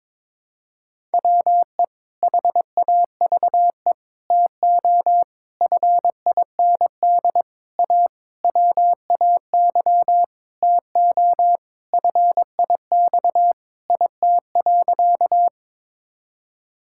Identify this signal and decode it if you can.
Morse code, heard as a keyed tone, about 22 wpm, 700 Hz. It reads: WE HAVE TO FIND A WAY TO FIX IT.
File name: Sentences with 7-Word Phrases - 22wpm